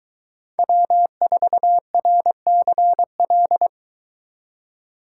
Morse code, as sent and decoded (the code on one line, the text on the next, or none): .-- ....- .-. -.-. .-..
W4RCL